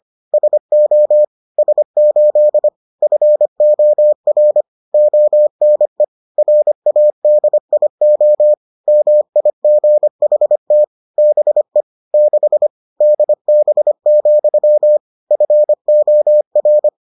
Morse code, heard as a keyed tone, about 25 wpm, 600 Hz. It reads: SO S8 FOR ONE RADIO MIGHT BE 6 DB, FOR